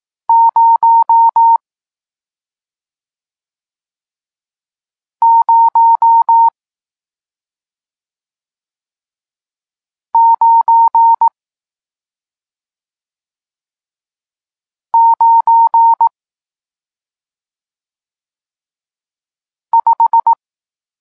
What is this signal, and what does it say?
Morse code, heard as a keyed tone, about 18 wpm, spaced as for 2 wpm, 920 Hz. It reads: 00995
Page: Tekst 40 – Nauka znaków przestankowych